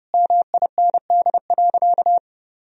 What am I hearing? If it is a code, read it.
Morse code, heard as a keyed tone, about 30 wpm, 700 Hz. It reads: MIND.
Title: Sentences from Top 1000 Words - Review Part 3 - 30wpm